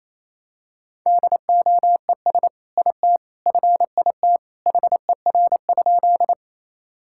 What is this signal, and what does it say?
Morse code, heard as a keyed tone, about 28 wpm, 700 Hz. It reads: DOES IT FIT HER?
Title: Sentences from Top 1000 Words - Review Part 5 - 28wpm